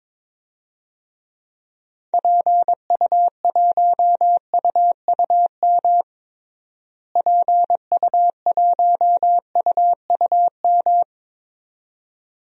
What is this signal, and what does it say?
Morse code, heard as a keyed tone, about 22 wpm, 700 Hz. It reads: PU1UUM PU1UUM
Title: DXCC-100 Call Signs - 6 Characters - Repeated 2x - 22wpm